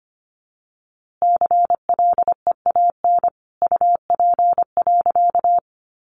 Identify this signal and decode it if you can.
Morse code, heard as a keyed tone, about 25 wpm, 700 Hz. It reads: CLEAN UP.
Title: A Sound Of Thunder 25wpm